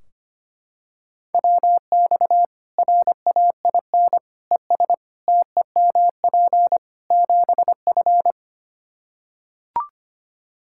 Morse code, heard as a keyed tone, about 25 wpm, 700 Hz. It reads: WX RAIN ES TEMP 7F
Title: Practice QSO - Exchange 2 - Info - 25wpm